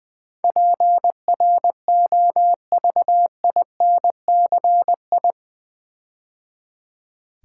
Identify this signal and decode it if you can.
Morse code, heard as a keyed tone, about 20 wpm, 700 Hz. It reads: PROVINCI
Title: Build A Word - 4 Letters At A Time - 10-Letter Words - 20wpm